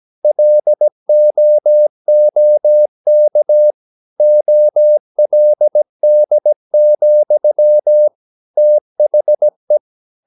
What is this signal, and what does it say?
Morse code, heard as a keyed tone, about 17 wpm, 600 Hz. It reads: LOOK OLD, THE
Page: Romanen Never af Ken Follet på engelsk.